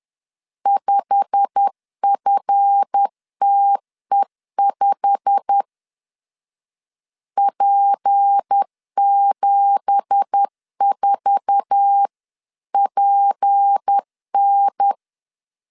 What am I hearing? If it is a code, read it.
Morse code, heard as a keyed tone, about 10 wpm, 800 Hz. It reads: 5FTE5 P74PN